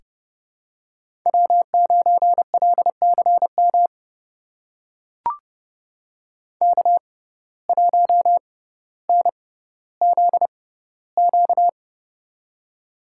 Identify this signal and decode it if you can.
Morse code, heard as a keyed tone, about 30 wpm, spaced as for 15 wpm, 700 Hz. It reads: W9LCM K1NZQ
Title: USA Call Signs - 5 Characters - 6x Character Spacing - 30wpm